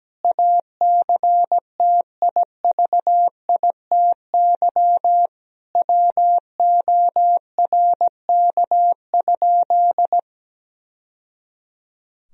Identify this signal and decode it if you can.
Morse code, heard as a keyed tone, about 17 wpm, 700 Hz. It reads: ACTIVITY WORK?
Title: Sentences from Top 1000 Words - Review Part 4 - 17wpm